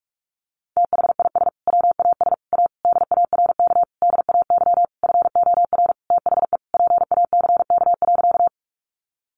Morse code, heard as a keyed tone, about 46 wpm, 700 Hz. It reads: THIS WAS A DARK DAY FOR THE PACK.
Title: Sentences from Top 1300 Words 45wpm